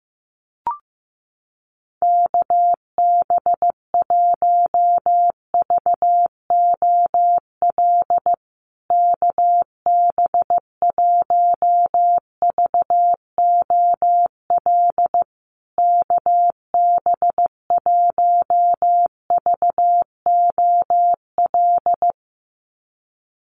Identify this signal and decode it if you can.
Morse code, heard as a keyed tone, about 15 wpm, 700 Hz. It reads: KB1VOL KB1VOL KB1VOL
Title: USA Call Signs Repeated 3x 15wpm